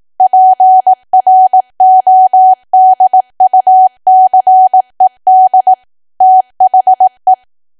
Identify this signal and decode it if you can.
Morse code, heard as a keyed tone, about 18 wpm, 750 Hz. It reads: PRODUCED THE
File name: Archive / 18 WPM / 18WPM